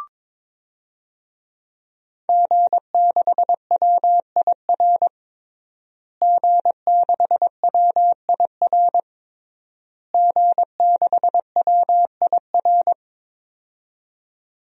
Morse code, heard as a keyed tone, about 22 wpm, 700 Hz. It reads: G6WIR G6WIR G6WIR
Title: DXCC-100 Call Signs - 5 Characters - Repeated 3x - 22wpm